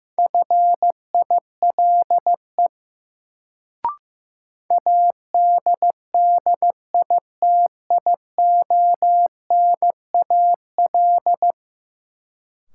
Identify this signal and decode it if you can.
Morse code, heard as a keyed tone, about 15 wpm, 700 Hz. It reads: FILE ADDITIONAL
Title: Sentences from Top 1000 Words - Review Part 4 - New Word Drill - 15wpm